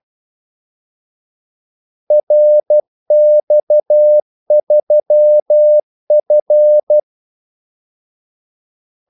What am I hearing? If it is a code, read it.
Morse code, heard as a keyed tone, about 12 wpm, 600 Hz. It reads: RX3F
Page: Kaldesignaler 12 wpm